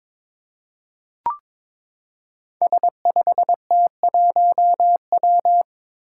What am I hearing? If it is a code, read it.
Morse code, heard as a keyed tone, about 22 wpm, 700 Hz. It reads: S5T1W
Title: Five Challenging Letter-Number Combinations 22wpm